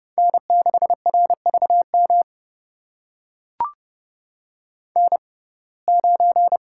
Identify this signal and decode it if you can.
Morse code, heard as a keyed tone, about 30 wpm, spaced as for 22 wpm, 700 Hz. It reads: N6RVM N9
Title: USA Call Signs - 5 Characters - 6x Character Spacing - 30wpm